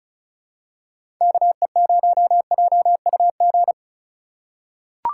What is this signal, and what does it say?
Morse code, heard as a keyed tone, about 36 wpm, 700 Hz. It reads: KE0JUG
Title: USA Call Signs - 6 Characters - 4x Character Spacing - 35wpm